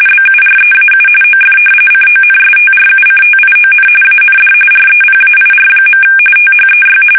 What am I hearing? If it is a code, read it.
RTTY, about 45.45 baud. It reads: BUREAU OF CONSTRUCTION AND REPAIR. THEY
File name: fsk850 / battleshipnj